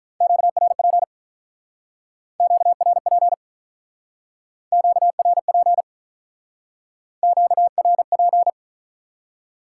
Morse code, heard as a keyed tone, about 47 wpm, 700 Hz. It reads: QRP QRP QRP QRP